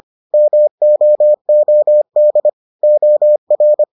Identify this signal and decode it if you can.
Morse code, heard as a keyed tone, about 25 wpm, 600 Hz. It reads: MOOD OR